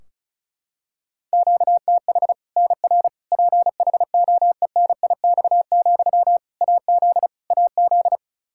Morse code, heard as a keyed tone, about 36 wpm, 700 Hz. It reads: QTH NR PHOENIX, AZ AZ
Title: K1USN SST QSO Practice 35wpm